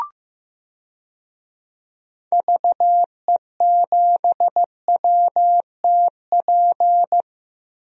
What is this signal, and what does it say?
Morse code, heard as a keyed tone, about 16 wpm, 700 Hz. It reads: VE7WTP
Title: DXCC-100 Call Signs - 6 Characters - Repeated 3x - 15wpm